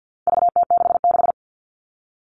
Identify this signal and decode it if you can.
Morse code, heard as a keyed tone, about 50 wpm, 700 Hz. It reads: VT66
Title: All Cut Numbers - 4-Digit 50wpm